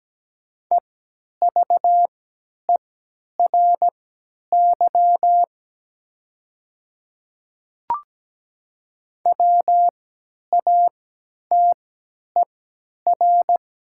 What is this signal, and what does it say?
Morse code, heard as a keyed tone, about 17 wpm, spaced as for 10 wpm, 700 Hz. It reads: EVERY WATER
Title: 3x Character Spacing - Top 200 Words - 17wpm